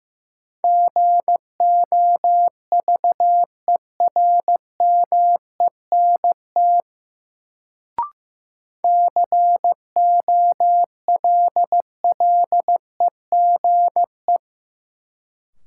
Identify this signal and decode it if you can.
Morse code, heard as a keyed tone, about 15 wpm, 700 Hz. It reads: GOVERMENT COLLEGE
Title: Sentences from Top 1300 Words - New Word Drill - 15wpm